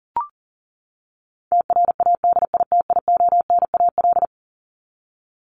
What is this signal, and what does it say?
Morse code, heard as a keyed tone, about 40 wpm, 700 Hz. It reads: TRADITIONAL
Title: Top 751-1000 Words 40wpm